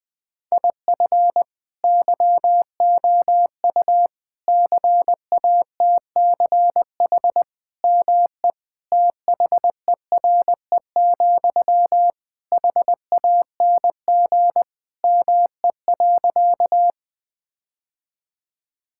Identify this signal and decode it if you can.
Morse code, heard as a keyed tone, about 20 wpm, 700 Hz. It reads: IF YOU CATCH ME THERE, HANG ME.
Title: Sentences from Top 1600 Words 20wpm